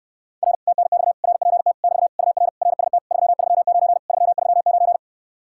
Morse code, heard as a keyed tone, about 69 wpm, 700 Hz. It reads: R TNX RPT <BT> UR RST 339 339